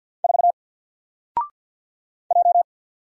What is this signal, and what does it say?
Morse code, heard as a keyed tone, about 48 wpm, 700 Hz. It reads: 4 J
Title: Single Letter-Number Alt 50wpm